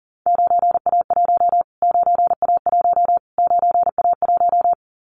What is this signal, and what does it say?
Morse code, heard as a keyed tone, about 40 wpm, 700 Hz. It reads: 9A1 9A1 9A1
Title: Callsign Prefixes for Top 25 DXCC - Repeated 3x - 40wpm